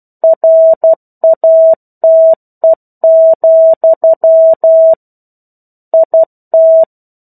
Morse code, heard as a keyed tone, about 12 wpm, 650 Hz. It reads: RATE, IT
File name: Chapter-01 12 WPM